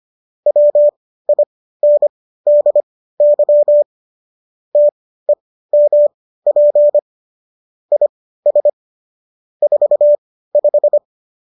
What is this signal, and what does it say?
Morse code, heard as a keyed tone, about 25 wpm, spaced as for 15 wpm, 590 Hz. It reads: WINDY TEMP IS 45